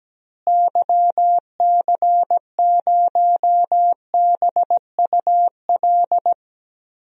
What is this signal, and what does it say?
Morse code, heard as a keyed tone, about 17 wpm, 700 Hz. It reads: YC0BUL
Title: DXCC-100 Call Signs - 6 Characters - Repeated 3x - 17wpm